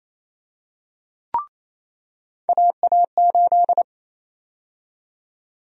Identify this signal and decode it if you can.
Morse code, heard as a keyed tone, about 27 wpm, 700 Hz. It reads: AA8
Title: USA Call Sign Prefixes - 28wpm